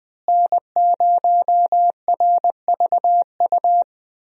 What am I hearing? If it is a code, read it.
Morse code, heard as a keyed tone, about 21 wpm, 700 Hz. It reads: N0RVU
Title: USA Call Signs - 5 Characters - Repeated 3x - 20wpm